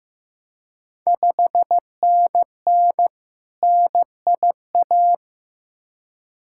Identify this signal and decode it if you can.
Morse code, heard as a keyed tone, about 15 wpm, 700 Hz. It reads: 5NN NIA
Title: Ontario QSO Party Exchange 15wpm